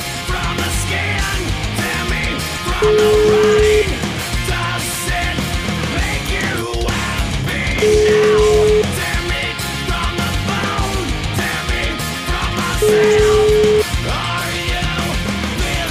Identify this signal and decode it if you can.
Ringback tone (Europe, 425 Hz), with noise.